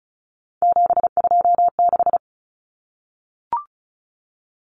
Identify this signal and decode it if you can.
Morse code, heard as a keyed tone, about 34 wpm, 700 Hz. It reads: 726
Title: Common Cut Numbers - 3-Digit 35wpm